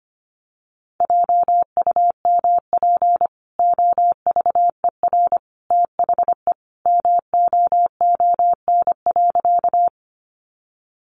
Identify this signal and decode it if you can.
Morse code, heard as a keyed tone, about 25 wpm, 700 Hz. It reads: JUMP OVER THE MOON.
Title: Sentences from Top 1300 Words 25wpm